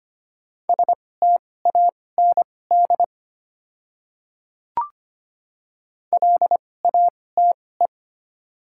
Morse code, heard as a keyed tone, about 25 wpm, spaced as for 18 wpm, 690 Hz. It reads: STAND LATE
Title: 2x Character Spacing - Top 500 Words - 25wpm